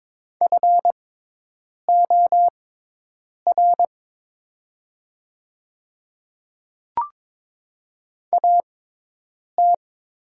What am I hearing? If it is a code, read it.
Morse code, heard as a keyed tone, about 22 wpm, spaced as for 8 wpm, 700 Hz. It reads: FOR AT
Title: 6x Character Spacing - Top 200 Words - 22wpm